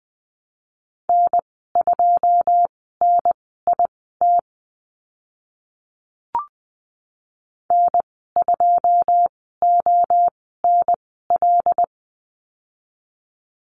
Morse code, heard as a keyed tone, about 20 wpm, spaced as for 15 wpm, 700 Hz. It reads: N2NIT N2ONL
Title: 082 Lesson 14g - Number 2 - Call Signs 20wpm